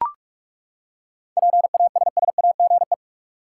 Morse code, heard as a keyed tone, about 42 wpm, 700 Hz. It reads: PASSAGE
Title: Seven Letter Words - 45wpm